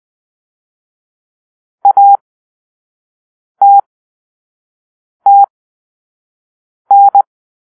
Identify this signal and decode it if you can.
Morse code, heard as a keyed tone, about 20 wpm, spaced as for 5 wpm, 800 Hz. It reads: ATTN